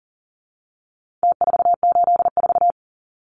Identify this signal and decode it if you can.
Morse code, heard as a keyed tone, about 40 wpm, 700 Hz. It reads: T484
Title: Numbers - Contest Style - 4-Digit 40wpm